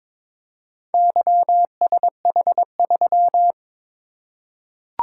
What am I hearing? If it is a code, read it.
Morse code, heard as a keyed tone, about 22 wpm, 700 Hz. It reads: YSH3
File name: practice / 22wpm / Four Letter-Number Combinations - 22wpm